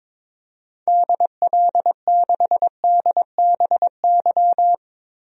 Morse code, heard as a keyed tone, about 22 wpm, 700 Hz. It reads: DL6DBY
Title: DXCC-100 Call Signs - 6 Characters - Repeated 3x - 22wpm